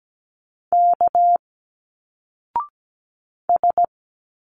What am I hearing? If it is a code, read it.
Morse code, heard as a keyed tone, about 17 wpm, 700 Hz. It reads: K S
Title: Single Letters 17wpm